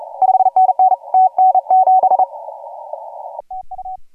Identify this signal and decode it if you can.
Morse code, heard as a keyed tone, about 36 wpm, 750 Hz. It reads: 5NN TN7 TU
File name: AUDIO2018 / HG1G CW